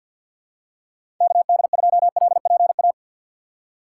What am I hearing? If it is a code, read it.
Morse code, heard as a keyed tone, about 46 wpm, 700 Hz. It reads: KD2LPA
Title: USA Call Signs - 6 Characters - 3x Character Spacing - 50wpm